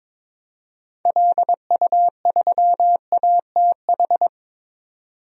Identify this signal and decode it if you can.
Morse code, heard as a keyed tone, about 22 wpm, 700 Hz. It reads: LU3ATH
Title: DXCC-100 Call Signs - 6 Characters - 22wpm